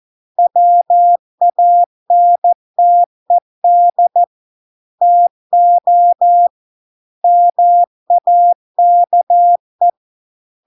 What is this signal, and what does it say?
Morse code, heard as a keyed tone, about 14 wpm, 700 Hz. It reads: WANTED TO MAKE